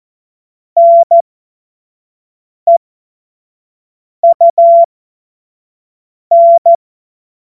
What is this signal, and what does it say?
Morse code, heard as a keyed tone, about 14 wpm, spaced as for 5 wpm, 680 Hz. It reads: NEUN
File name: Morsekurs / Witze / Witz 05. April 2022 25 F70